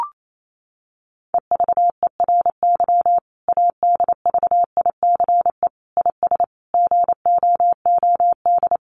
Morse code, heard as a keyed tone, about 28 wpm, 700 Hz. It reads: EVERY ADVICE IS GOOD
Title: Sentences from Top 1100 Words 28wpm